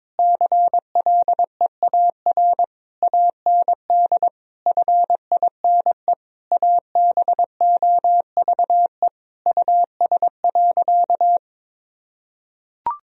Morse code, heard as a keyed tone, about 22 wpm, 700 Hz. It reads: CLEAR AND FINE ABOVE US.
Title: Sentences from Top 1000 Words - Review Part 2 - 22wpm